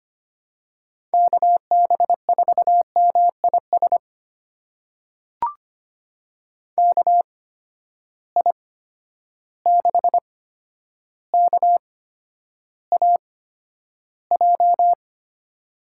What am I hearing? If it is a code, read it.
Morse code, heard as a keyed tone, about 25 wpm, spaced as for 11 wpm, 700 Hz. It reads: KB4MIS KI6KAJ
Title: USA Call Signs - 6 Characters - 8x Character Spacing - 25wpm